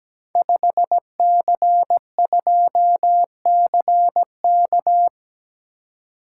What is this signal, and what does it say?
Morse code, heard as a keyed tone, about 17 wpm, 700 Hz. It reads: 5C2CK
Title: Code Groups - 5 Letter:Numbers - 17wpm